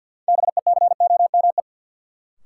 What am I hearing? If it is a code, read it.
Morse code, heard as a keyed tone, about 53 wpm, 700 Hz. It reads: BECOME
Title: Sets of 5 Words Encore - New Word Drill - 50wpm